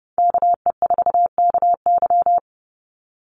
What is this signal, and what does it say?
Morse code, heard as a keyed tone, about 30 wpm, 700 Hz. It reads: KE4KY
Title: Call Signs Repeated 2x - 30wpm